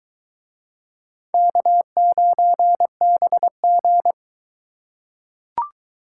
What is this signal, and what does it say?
Morse code, heard as a keyed tone, about 22 wpm, 700 Hz. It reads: K9BG